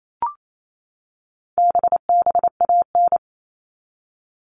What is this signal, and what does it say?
Morse code, heard as a keyed tone, about 28 wpm, 700 Hz. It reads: BBAN
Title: All Cut Numbers - 4-Digit 28wpm